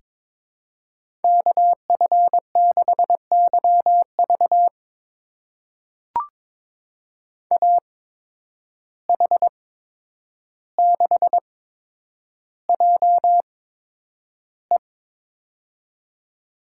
Morse code, heard as a keyed tone, about 22 wpm, spaced as for 9 wpm, 700 Hz. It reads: KF6YV AH6JE
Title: USA Call Signs - 5 Characters - 8x Character Spacing - 22wpm